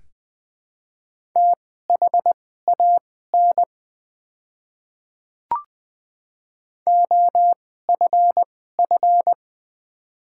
Morse code, heard as a keyed tone, about 20 wpm, spaced as for 14 wpm, 700 Hz. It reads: THAN OFF
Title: 2x Character Spacing - Top 200 Words - 20wpm